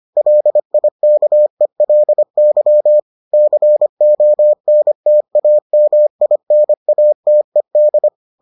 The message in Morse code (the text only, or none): LIKELY CONTAMINATED